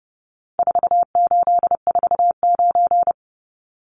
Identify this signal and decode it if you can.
Morse code, heard as a keyed tone, about 30 wpm, 700 Hz. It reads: 4849
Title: Numbers - 4-Digit 30wpm